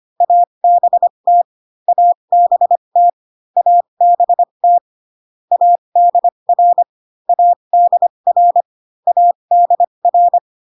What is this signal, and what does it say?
Morse code, heard as a keyed tone, about 25 wpm, spaced as for 22 wpm, 700 Hz. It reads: ABT ABT ABT ADR ADR ADR
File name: CW common abbreviations alphabetical repeated 3 times 25 18wpm